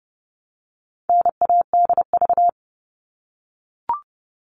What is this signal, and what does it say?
Morse code, heard as a keyed tone, about 30 wpm, 700 Hz. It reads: NADV
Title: All Cut Numbers - 4-Digit 30wpm